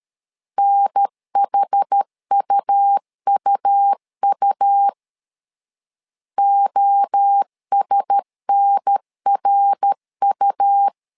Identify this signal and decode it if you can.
Morse code, heard as a keyed tone, about 13 wpm, 800 Hz. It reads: NHUUU OSNRU
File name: LEZIONI / 4 Q1S 12wpm 1802